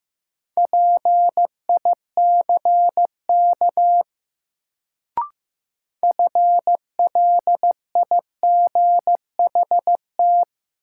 Morse code, heard as a keyed tone, about 15 wpm, 700 Hz. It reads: PICK FLIGHT
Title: Sentences from Top 1000 Words - New Word Drill - 15wpm